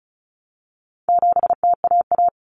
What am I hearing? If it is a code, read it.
Morse code, heard as a keyed tone, about 34 wpm, 700 Hz. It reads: 7TAA
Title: Common Cut Numbers - 4-Digit 35wpm